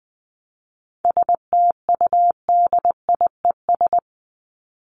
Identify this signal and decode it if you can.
Morse code, heard as a keyed tone, about 20 wpm, 700 Hz. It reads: STUDIES
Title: ES:S Suffix - Top 100 Words with Base Words 20wpm